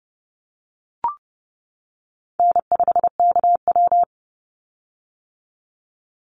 Morse code, heard as a keyed tone, about 30 wpm, 700 Hz. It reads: N5KW
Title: Call Signs - 30wpm